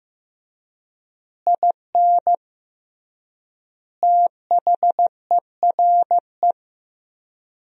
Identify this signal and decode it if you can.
Morse code, heard as a keyed tone, about 15 wpm, 700 Hz. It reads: IN THERE
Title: 3x Word-Spacing - Sentences from Top 500 Words - 15wpm